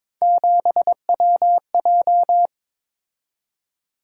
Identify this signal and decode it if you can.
Morse code, heard as a keyed tone, about 22 wpm, 700 Hz. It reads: 7WJ
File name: practice / 22wpm / Three Letter-Number Combinations - 22wpm